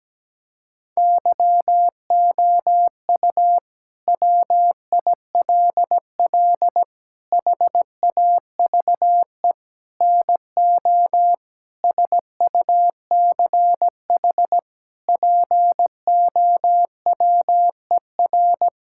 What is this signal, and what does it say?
Morse code, heard as a keyed tone, about 17 wpm, 700 Hz. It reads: YOU WILL HAVE NO SUCH POWER
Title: Sentences from Top 1000 Words - Review Part 6 - 17wpm